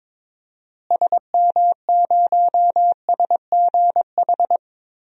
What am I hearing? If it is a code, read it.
Morse code, heard as a keyed tone, about 22 wpm, 700 Hz. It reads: SM0SGH
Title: DXCC-100 Call Signs - 6 Characters - Repeated 2x - 22wpm